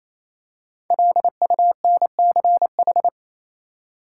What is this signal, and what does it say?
Morse code, heard as a keyed tone, about 28 wpm, 700 Hz. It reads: LUNCH
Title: Five Letter Words - 28wpm